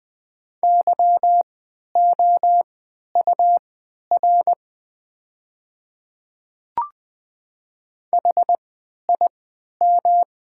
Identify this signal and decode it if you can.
Morse code, heard as a keyed tone, about 20 wpm, spaced as for 11 wpm, 700 Hz. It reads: YOUR HIM
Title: 3x Character Spacing - Top 100 Words - 20wpm